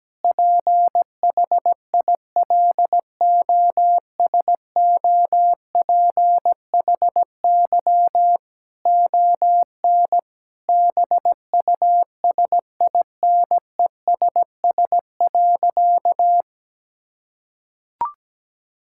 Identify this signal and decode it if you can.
Morse code, heard as a keyed tone, about 17 wpm, 700 Hz. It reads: PHILOSOPHY ON BUSINESS.
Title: Sentences from Top 1000 Words - Review Part 6 - 17wpm